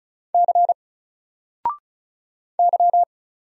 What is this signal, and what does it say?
Morse code, heard as a keyed tone, about 34 wpm, 700 Hz. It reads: C Y
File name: Single-Letter-Number-35wpm